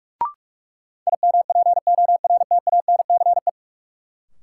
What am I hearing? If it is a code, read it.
Morse code, heard as a keyed tone, about 43 wpm, 700 Hz. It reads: IMPORTANCE
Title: Sentences from Top 1200 Words - New Word Drill Alt - 45wpm